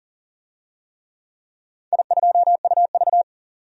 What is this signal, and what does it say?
Morse code, heard as a keyed tone, about 38 wpm, 700 Hz. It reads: I2UV
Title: DXCC-100 Call Signs - 4 Characters - 40wpm